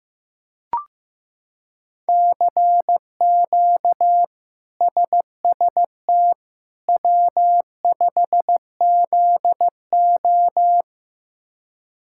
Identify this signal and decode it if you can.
Morse code, heard as a keyed tone, about 15 wpm, 700 Hz. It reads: CQ SST W5ZO
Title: K1USN SST QSO Practice - 15wpm